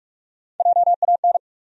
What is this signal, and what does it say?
Morse code, heard as a keyed tone, about 46 wpm, 700 Hz. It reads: JAN
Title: USA Common Names - Max 5 Letters - 45wpm